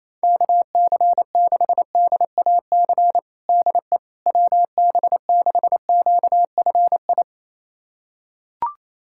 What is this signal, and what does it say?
Morse code, heard as a keyed tone, about 28 wpm, 700 Hz. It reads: KC6DAC DE WB6QFI
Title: USA Callsign DE Callsign - 6 Characters - 28wpm